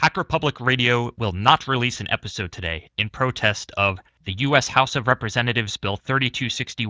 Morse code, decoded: none